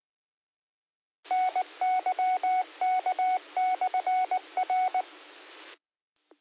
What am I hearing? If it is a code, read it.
Morse code, heard as a keyed tone, about 19 wpm, 740 Hz. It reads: NYK/R